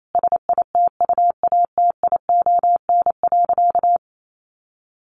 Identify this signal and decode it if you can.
Morse code, heard as a keyed tone, about 28 wpm, 700 Hz. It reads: SITUATION.
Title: Sentences from Top 1200 Words 28wpm